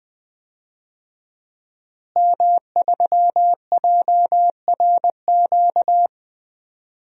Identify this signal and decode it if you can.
Morse code, heard as a keyed tone, about 21 wpm, 700 Hz. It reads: M3JRQ
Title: DXCC-100 Call Signs - 5 Characters - Repeated 2x - 20wpm